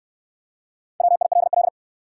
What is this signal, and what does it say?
Morse code, heard as a keyed tone, about 61 wpm, 700 Hz. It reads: WILL